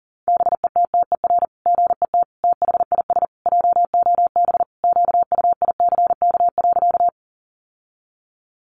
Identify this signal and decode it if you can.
Morse code, heard as a keyed tone, about 37 wpm, 700 Hz. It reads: BETTER GET THIS JOB QUICK.
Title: Sentences from Top 600 Words 40wpm